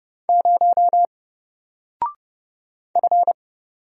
Morse code, heard as a keyed tone, about 30 wpm, 700 Hz. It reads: 0 F
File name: Single-Letter-Number-30wpm